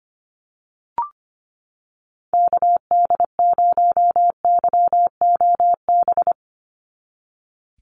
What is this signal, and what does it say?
Morse code, heard as a keyed tone, about 25 wpm, 700 Hz. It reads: KD0YOB
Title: NA SOTA Activator Callsigns 25wpm